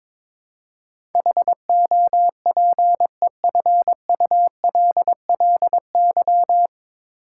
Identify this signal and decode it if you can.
Morse code, heard as a keyed tone, about 22 wpm, 700 Hz. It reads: HOPEFULLY
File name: 9 Letter Words - First 3 Letters Given - 22wpm